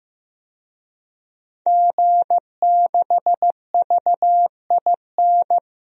Morse code, heard as a keyed tone, about 16 wpm, 700 Hz. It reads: G6VIN
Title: DXCC-100 Call Signs - 5 Characters - 15wpm